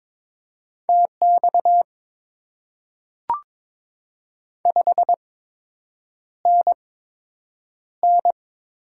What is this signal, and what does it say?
Morse code, heard as a keyed tone, about 22 wpm, spaced as for 8 wpm, 700 Hz. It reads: TX 5NN
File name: Top 90 QSO Elements - 8x Character Spacing - 22wpm